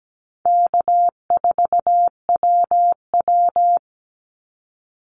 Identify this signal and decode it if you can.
Morse code, heard as a keyed tone, about 17 wpm, 700 Hz. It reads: K4WW
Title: Easy Call Signs Repeated 3x - 17wpm